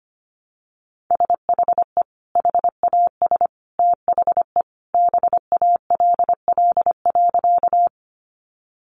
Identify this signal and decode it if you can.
Morse code, heard as a keyed tone, about 25 wpm, 700 Hz. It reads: SHE HAS THE BALL.
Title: Sentences from Top 900 Words 25wpm